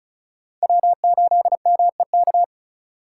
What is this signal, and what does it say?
Morse code, heard as a keyed tone, about 35 wpm, 700 Hz. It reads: W8MEK
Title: Speed Racing - USA 1x3 Call Signs 30wpm